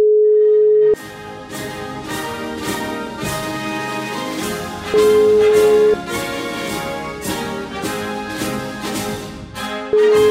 Ringback tone (Europe, 425 Hz), with noise.